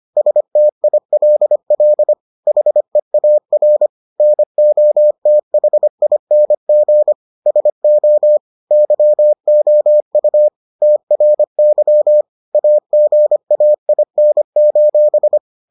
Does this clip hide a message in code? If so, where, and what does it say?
Yes: Morse code from 0.2 to 15.4 s, heard as a keyed tone, about 25 wpm, 600 Hz. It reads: STILL HEAR NOTHING SO YOU TRY AGAIN: